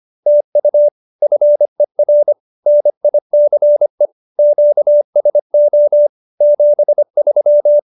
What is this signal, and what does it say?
Morse code, heard as a keyed tone, about 25 wpm, 590 Hz. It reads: TU FER NICE QSO 73